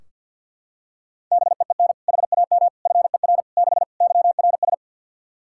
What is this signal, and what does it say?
Morse code, heard as a keyed tone, about 46 wpm, 700 Hz. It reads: BEEN HAM FER 6 YRS
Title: K1USN SST QSO Practice 50wpm